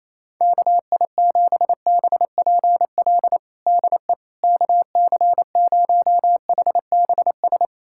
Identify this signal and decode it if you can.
Morse code, heard as a keyed tone, about 28 wpm, 700 Hz. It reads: KI7BPL DE KC0HBS
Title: USA Callsign DE Callsign - 6 Characters - 28wpm